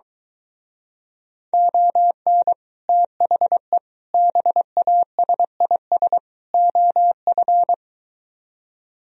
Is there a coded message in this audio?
Yes: Morse code from 1.5 to 7.7 s, heard as a keyed tone, about 22 wpm, 700 Hz. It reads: ON THE BASIS OF